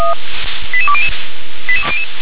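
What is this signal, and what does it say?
Five-tone selective call (ZVEI), heard as quick runs of five tones, tones of about 70 ms. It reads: 88200
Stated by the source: ZVEI five-tone selective call received on SDR.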